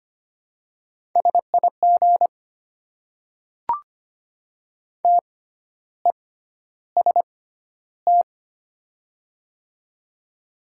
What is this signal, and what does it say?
Morse code, heard as a keyed tone, about 25 wpm, spaced as for 12 wpm, 700 Hz. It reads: SIG TEST
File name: Top 90 QSO Elements - 6x Character Spacing - 25wpm